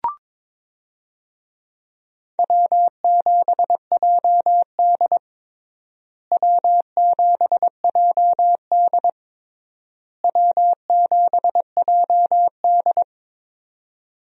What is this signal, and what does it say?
Morse code, heard as a keyed tone, about 22 wpm, 700 Hz. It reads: W7JD W7JD W7JD
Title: USA Call Signs - 4 Characters - Repeated 3x - 22wpm